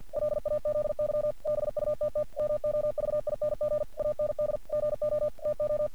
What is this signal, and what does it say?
Morse code, heard as a keyed tone, about 48 wpm, 600 Hz. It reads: LAZY BUTT MOVING AND GO TO
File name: archived / CRSnet